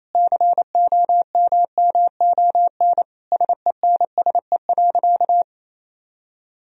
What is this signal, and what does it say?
Morse code, heard as a keyed tone, about 27 wpm, 700 Hz. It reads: COMMON SENSE.
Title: Sentences from Top 1000 Words - Review Part 4 - 28wpm